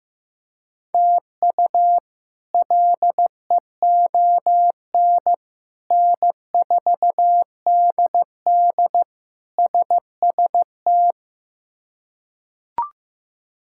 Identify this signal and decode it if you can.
Morse code, heard as a keyed tone, about 15 wpm, 700 Hz. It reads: TU LEON N4DD SST
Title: K1USN SST QSO Practice - 15wpm